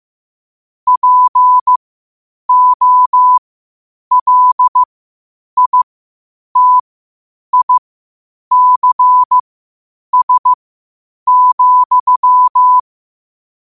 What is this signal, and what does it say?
Morse code, heard as a keyed tone, about 15 wpm, spaced as for 8 wpm, 1000 Hz. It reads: POLITICS,
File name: qotdfiles / 8wpm